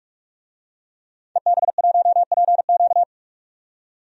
Morse code, heard as a keyed tone, about 46 wpm, 700 Hz. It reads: EB1PQ
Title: DXCC-100 Call Signs - 5 Characters - 45wpm